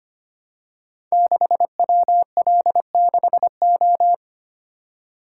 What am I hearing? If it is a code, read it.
Morse code, heard as a keyed tone, about 25 wpm, 700 Hz. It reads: 6WL6O
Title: Code Groups - 5 Letter:Numbers - 25wpm